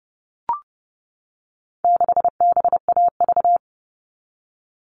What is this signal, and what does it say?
Morse code, heard as a keyed tone, about 30 wpm, 700 Hz. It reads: 6BAV
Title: All Cut Numbers - 4-Digit 30wpm